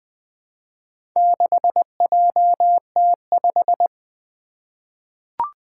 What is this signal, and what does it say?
Morse code, heard as a keyed tone, about 20 wpm, 700 Hz. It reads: 6JT5
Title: Four Challenging Letter-Number Combinations 20wpm